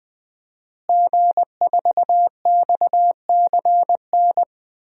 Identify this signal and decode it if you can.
Morse code, heard as a keyed tone, about 20 wpm, 700 Hz. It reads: G4XCN
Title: DXCC-100 Call Signs - 5 Characters - Repeated 2x - 20wpm